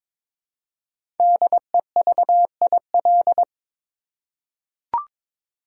Morse code, heard as a keyed tone, about 22 wpm, 690 Hz. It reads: DEVIL
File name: 5 Letter Words - First 3 Letters Given - 22wpm